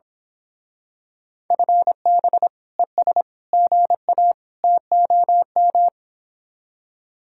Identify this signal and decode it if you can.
Morse code, heard as a keyed tone, about 25 wpm, 700 Hz. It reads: FB ES GA TOM
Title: Speed-Racing Common QSO Phrases - 22wpm